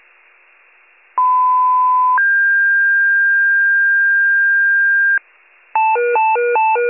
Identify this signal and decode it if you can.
Two-tone page, heard as a one-second tone, then a three-second tone, about 1010 then 1600 Hz. It is heard from 1.2 to 5.2 s.
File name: wcalert